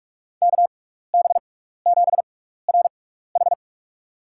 Morse code, heard as a keyed tone, about 45 wpm, spaced as for 15 wpm, 700 Hz. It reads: KB7RH
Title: USA Call Signs - 5 Characters - 6x Character Spacing - 45wpm